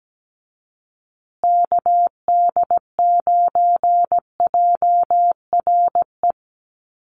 Morse code, heard as a keyed tone, about 17 wpm, 700 Hz. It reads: KD9JRE
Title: USA Call Signs Repeated 2x 17wpm